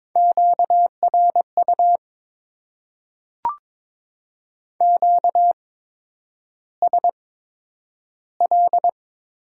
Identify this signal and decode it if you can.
Morse code, heard as a keyed tone, about 22 wpm, spaced as for 9 wpm, 700 Hz. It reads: QRU QSL
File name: Top 90 QSO Elements - 8x Character Spacing - 22wpm